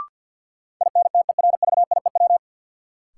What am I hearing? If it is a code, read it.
Morse code, heard as a keyed tone, about 48 wpm, 700 Hz. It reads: INTERVIEW
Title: Sentences from Top 1600 Words - New Word Drill Alt - 50wpm